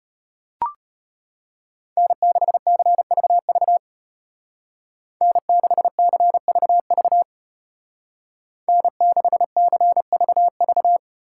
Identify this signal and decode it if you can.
Morse code, heard as a keyed tone, about 33 wpm, 700 Hz. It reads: N6CVV N6CVV N6CVV